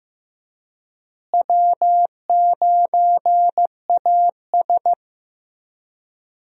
Morse code, heard as a keyed tone, about 15 wpm, 700 Hz. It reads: W9AS
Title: USA Call Signs - 4 Characters - Repeated 3x - 15wpm